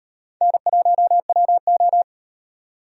Morse code, heard as a keyed tone, about 38 wpm, 700 Hz. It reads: N1WO